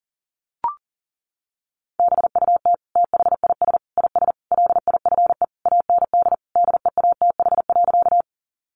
Morse code, heard as a keyed tone, about 40 wpm, 700 Hz. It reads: BUT THIS IS LIFE AND DEATH.
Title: Sentences from Top 1100 Words 40wpm